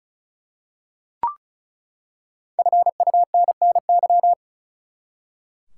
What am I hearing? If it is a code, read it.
Morse code, heard as a keyed tone, about 34 wpm, 700 Hz. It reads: FUNNY
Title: Five Letter Words - 35wpm